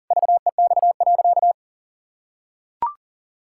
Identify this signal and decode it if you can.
Morse code, heard as a keyed tone, about 40 wpm, 700 Hz. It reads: VEX.
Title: Pangram Challenge v2 - 40wpm